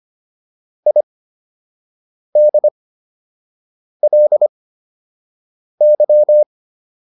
Morse code, heard as a keyed tone, about 25 wpm, spaced as for 6 wpm, 600 Hz. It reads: IDLY